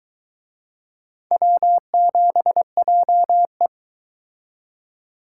Morse code, heard as a keyed tone, about 22 wpm, 700 Hz. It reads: W7JE